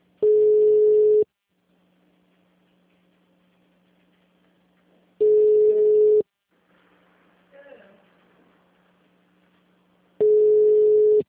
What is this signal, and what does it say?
Ringback tone (Europe, 425 Hz), with speech.